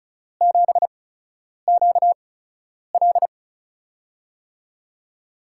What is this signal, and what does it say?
Morse code, heard as a keyed tone, about 36 wpm, spaced as for 10 wpm, 700 Hz. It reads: 7QL
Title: USA Call Signs - 5 Characters - 8x Character Spacing - 35wpm